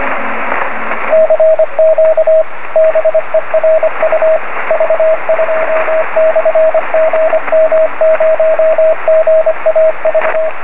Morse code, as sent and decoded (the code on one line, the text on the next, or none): -.-. --.- / -... .-. ..- / ...- ..--- -..-. --. -- ----- --. .- ...-
CQ BERU V2/GM0GAV